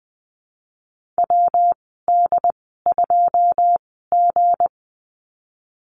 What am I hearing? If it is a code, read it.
Morse code, heard as a keyed tone, about 20 wpm, spaced as for 15 wpm, 700 Hz. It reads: WD2G
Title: 193 Lesson 31g - Letter v - Call Signs 20wpm